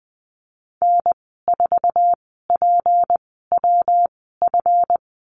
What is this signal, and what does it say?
Morse code, heard as a keyed tone, about 20 wpm, spaced as for 15 wpm, 700 Hz. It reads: N4PWF
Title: 179 Lesson 29g - Slash - Call Signs 20wpm